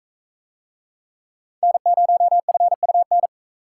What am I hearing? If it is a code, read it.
Morse code, heard as a keyed tone, about 43 wpm, 700 Hz. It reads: N0FUN